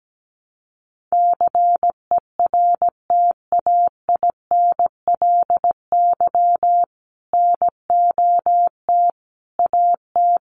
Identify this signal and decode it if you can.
Morse code, heard as a keyed tone, about 17 wpm, 700 Hz. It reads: CERTAINLY NOT AT
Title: Sentences from Top 900 Words 17wpm